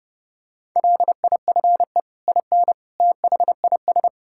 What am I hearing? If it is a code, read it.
Morse code, heard as a keyed tone, about 30 wpm, 700 Hz. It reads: LIFE IN THIS
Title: Sentences from Top 1000 Words - Review Part 5 - 30wpm